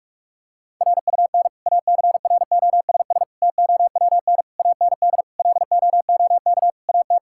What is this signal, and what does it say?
Morse code, heard as a keyed tone, about 46 wpm, 700 Hz. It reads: RUN ACROSS TOWN AND LOOK AT
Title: Sentences from Top 1000 Words - Review Part 5 - 45wpm